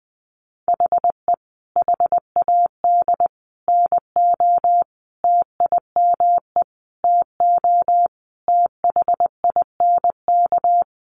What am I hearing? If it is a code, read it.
Morse code, heard as a keyed tone, about 20 wpm, 700 Hz. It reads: HE HAD NO TIME TO THINK
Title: Sentences from Top 700 Words 20wpm